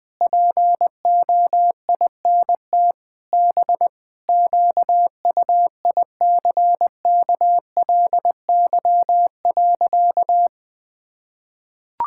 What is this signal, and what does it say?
Morse code, heard as a keyed tone, about 20 wpm, 700 Hz. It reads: POINT B QUICKLY.
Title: Sentences with 7-Word Phrases - Spoken Beginning - 20wpm